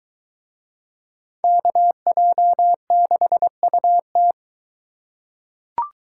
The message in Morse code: KJ6UT

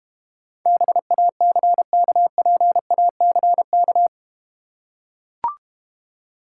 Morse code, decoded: BACKPACK

